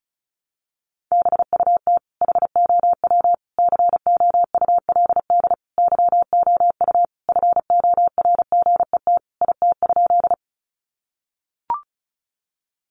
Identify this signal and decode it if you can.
Morse code, heard as a keyed tone, about 34 wpm, 700 Hz. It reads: BUT HOW COULD YOU FORGET IT?
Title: Sentences from Top 1200 Words 35wpm